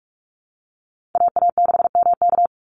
Morse code, heard as a keyed tone, about 44 wpm, 700 Hz. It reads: AA6MK
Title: Call Signs Repeated 3x - 45wpm